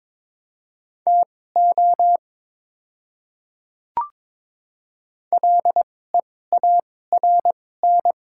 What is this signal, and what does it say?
Morse code, heard as a keyed tone, about 22 wpm, spaced as for 16 wpm, 700 Hz. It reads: TO LEARN
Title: 2x Character Spacing - Top 300 Words - 22wpm